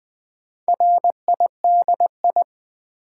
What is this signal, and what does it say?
Morse code, heard as a keyed tone, about 20 wpm, 700 Hz. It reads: RIDI
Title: Build A Word - 4 Letters At A Time - 10-Letter Words - 20wpm